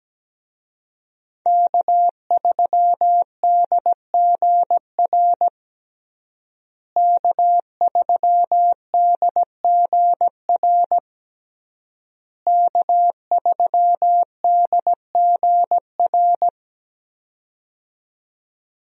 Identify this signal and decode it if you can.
Morse code, heard as a keyed tone, about 17 wpm, 700 Hz. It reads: K3DGR K3DGR K3DGR
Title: USA Call Signs - 5 Characters - Repeated 3x - 17wpm